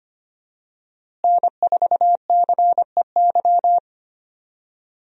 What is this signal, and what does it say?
Morse code, heard as a keyed tone, about 25 wpm, 700 Hz. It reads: N4CEY
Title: USA Call Signs - 5 Characters - Repeated 2x - 25wpm